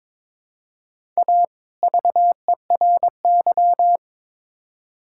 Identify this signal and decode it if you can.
Morse code, heard as a keyed tone, about 23 wpm, 700 Hz. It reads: A VERY
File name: Build A Phrase - 2-Word Phrases - 22wpm